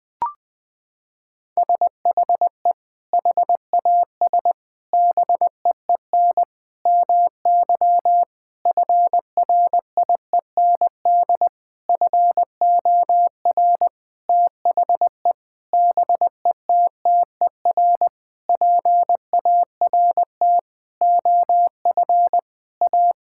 Morse code, heard as a keyed tone, about 20 wpm, 700 Hz. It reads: SHE HAS BEEN MY FRIEND FOR THE BETTER PART OF A
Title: Sentences with 6-Word Phrases - 20wpm